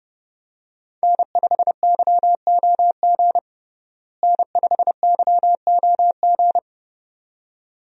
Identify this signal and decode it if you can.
Morse code, heard as a keyed tone, about 30 wpm, 700 Hz. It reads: N5YOG N5YOG
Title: USA Call Signs - 5 Characters - Repeated 2x - 30wpm